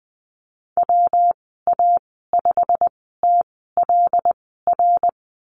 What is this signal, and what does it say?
Morse code, heard as a keyed tone, about 19 wpm, spaced as for 14 wpm, 700 Hz. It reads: WA5TLR
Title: 154 Lesson 25g - Letter p - Call Signs 20wpm